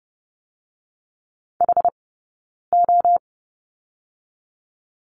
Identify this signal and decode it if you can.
Morse code, heard as a keyed tone, about 30 wpm, 700 Hz. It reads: H O
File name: Lesson 17d - Letter u - Two Characters 30wpm